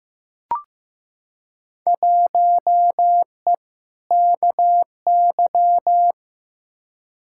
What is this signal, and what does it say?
Morse code, heard as a keyed tone, about 15 wpm, 700 Hz. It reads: 1E KY
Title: Field Day Exchanges - 15wpm